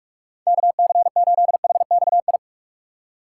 Morse code, heard as a keyed tone, about 47 wpm, 700 Hz. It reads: KC8HXI
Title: USA Call Signs - 6 Characters - 4x Character Spacing - 45wpm